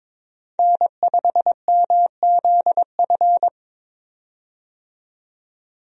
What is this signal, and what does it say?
Morse code, heard as a keyed tone, about 22 wpm, 700 Hz. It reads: N5MZF
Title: USA Call Signs - 5 Characters - 22wpm